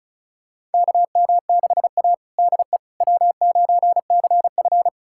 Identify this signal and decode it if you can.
Morse code, heard as a keyed tone, about 36 wpm, 700 Hz. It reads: KM6A DE W9CF
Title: USA Callsign DE Callsign - 4 Characters - 35wpm